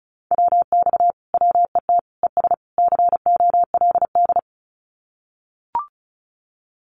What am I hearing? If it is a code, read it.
Morse code, heard as a keyed tone, about 34 wpm, 700 Hz. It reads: WX WET ES COLD
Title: Common QSO Phrases Encore 35wpm